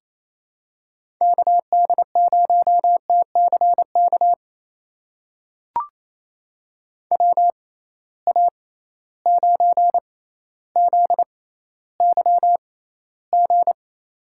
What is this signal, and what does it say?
Morse code, heard as a keyed tone, about 28 wpm, spaced as for 14 wpm, 700 Hz. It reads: KD0TCK WA9ZYG
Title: USA Call Signs - 6 Characters - 6x Character Spacing - 28wpm